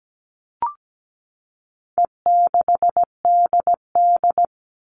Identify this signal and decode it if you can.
Morse code, heard as a keyed tone, about 17 wpm, 700 Hz. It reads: E6DD
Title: All Cut Numbers - 4-Digit 17wpm